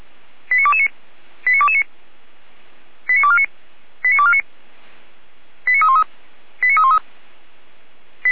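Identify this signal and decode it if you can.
Five-tone selective call (ZVEI), heard as quick runs of five tones, tones of about 70 ms. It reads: 89229 89229 89249 89249 89313 89313
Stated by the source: ZVEI five-tone call received by SDR.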